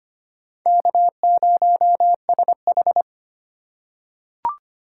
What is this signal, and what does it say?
Morse code, heard as a keyed tone, about 25 wpm, 700 Hz. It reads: K0SH
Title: USA Call Signs - 4 Characters - 4x Character Spacing - 25wpm